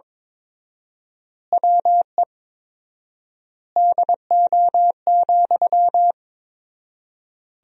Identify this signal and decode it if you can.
Morse code, heard as a keyed tone, about 22 wpm, 700 Hz. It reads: WE DO,
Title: 4x Word-Spacing - Sentences from Top 100 Words - 22wpm